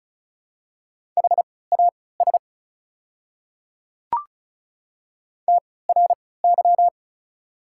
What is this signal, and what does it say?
Morse code, heard as a keyed tone, about 36 wpm, spaced as for 20 wpm, 690 Hz. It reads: HAS TRY
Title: 3x Character Spacing - Top 300 Words - 35wpm